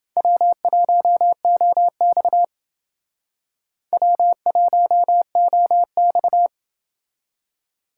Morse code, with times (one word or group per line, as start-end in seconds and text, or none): 0.2-2.4 W1OX
3.9-6.5 W1OX